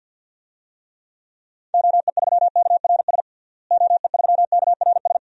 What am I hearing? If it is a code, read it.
Morse code, heard as a keyed tone, about 47 wpm, 700 Hz. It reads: OE3KRS OE3KRS
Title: DXCC-100 Call Signs - 6 Characters - Repeated 2x - 50wpm